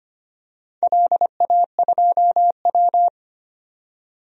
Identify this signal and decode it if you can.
Morse code, heard as a keyed tone, about 25 wpm, 700 Hz. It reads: LA2W
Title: DXCC-100 Call Signs - 4 Characters - 25wpm